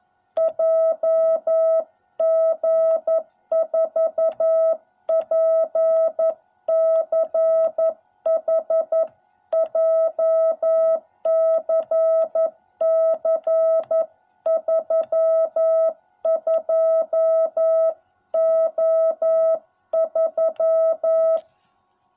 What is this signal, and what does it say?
Morse code, heard as a keyed tone, about 11 wpm, 660 Hz. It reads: JG4PCH JCC 32O3